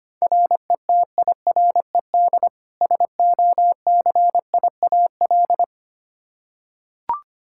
Morse code, heard as a keyed tone, about 25 wpm, 700 Hz. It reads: RETIRED SOCIAL
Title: Practice QSO - Exchange 3 - More Info - 25wpm